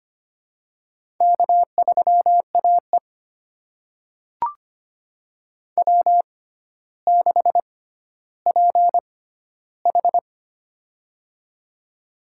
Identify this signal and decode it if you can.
Morse code, heard as a keyed tone, about 25 wpm, spaced as for 13 wpm, 700 Hz. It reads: K3AE W6PH
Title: USA Call Signs - 4 Characters - 6x Character Spacing - 25wpm